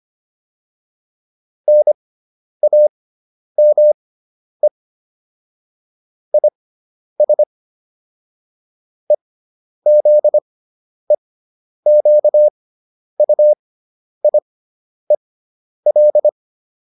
Morse code, heard as a keyed tone, about 25 wpm, spaced as for 10 wpm, 600 Hz. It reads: NAME IS EZEQUIEL